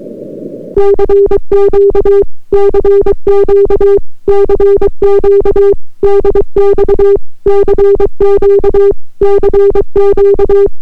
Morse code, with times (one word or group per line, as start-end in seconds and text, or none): 0.8-2.2 CQ
2.5-4.0 CQ
4.3-5.7 CQ
6.0-7.2 DX
7.5-8.9 CQ
9.2-10.7 CQ